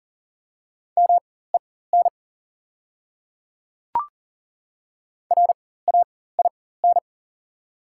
Morse code, heard as a keyed tone, about 42 wpm, spaced as for 19 wpm, 690 Hz. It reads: MEN RAIN
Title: 4x Character Spacing - Top 300 Words - 40wpm